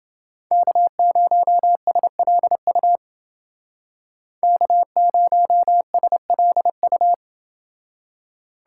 Morse code, heard as a keyed tone, about 29 wpm, 700 Hz. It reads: K0SLU K0SLU